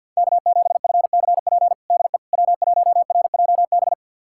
Morse code, heard as a keyed tone, about 51 wpm, 700 Hz. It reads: K7FCP DE W1RJB
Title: USA Callsign DE Callsign - 5 Characters - 50wpm